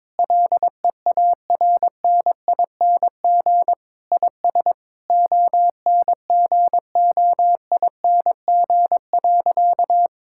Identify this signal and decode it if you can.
Morse code, heard as a keyed tone, about 22 wpm, 700 Hz. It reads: LEARNING IS ONGOING.
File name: practice / 22wpm / Sentences with 3-Word Phrases - 22wpm